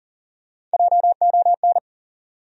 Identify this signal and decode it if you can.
Morse code, heard as a keyed tone, about 40 wpm, 700 Hz. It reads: JON
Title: USA Common Names - Max 4 Letters - 40wpm